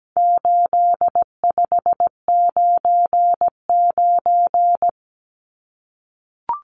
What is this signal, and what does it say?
Morse code, heard as a keyed tone, about 17 wpm, 700 Hz. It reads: 8599
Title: Numbers - 4-Digit 17wpm